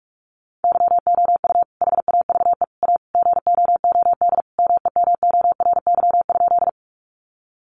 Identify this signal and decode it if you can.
Morse code, heard as a keyed tone, about 44 wpm, 700 Hz. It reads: YOU HAVE A GOOD MEMORY?
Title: Sentences from Top 1300 Words 45wpm